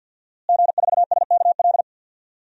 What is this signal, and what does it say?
Morse code, heard as a keyed tone, about 53 wpm, 700 Hz. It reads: G4SKL
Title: DXCC-100 Call Signs - 5 Characters - Repeated 3x - 50wpm